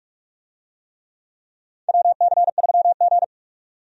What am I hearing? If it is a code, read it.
Morse code, heard as a keyed tone, about 44 wpm, 700 Hz. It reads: WC3G
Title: USA Call Signs - 4 Characters - 45wpm